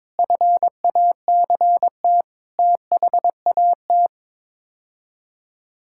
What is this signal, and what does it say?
Morse code, heard as a keyed tone, about 22 wpm, 700 Hz. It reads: FACT THAT
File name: practice / 22wpm / Sets of 7 Words - 22wpm